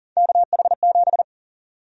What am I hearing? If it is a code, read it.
Morse code, heard as a keyed tone, about 40 wpm, 700 Hz. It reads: KH7
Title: USA Call Sign Prefixes - 40wpm